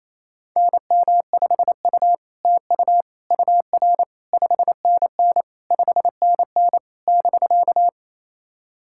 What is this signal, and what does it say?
Morse code, heard as a keyed tone, about 27 wpm, 700 Hz. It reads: NM5U TU UR 5NN 5NN <BK>
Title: SOTA QSO Practice 28wpm